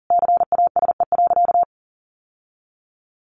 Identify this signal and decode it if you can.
Morse code, heard as a keyed tone, about 40 wpm, 700 Hz. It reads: CASE.
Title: Sentences from Top 1400 Words 40wpm